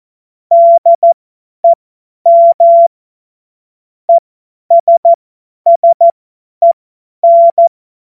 Morse code, heard as a keyed tone, about 14 wpm, spaced as for 10 wpm, 680 Hz. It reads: DEM ESSEN
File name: Witz-19-August-2025-50-F70